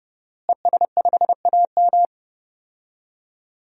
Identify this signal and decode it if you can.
Morse code, heard as a keyed tone, about 30 wpm, 700 Hz. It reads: ES5AM
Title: DXCC-100 Call Signs - 5 Characters - Repeated 3x - 30wpm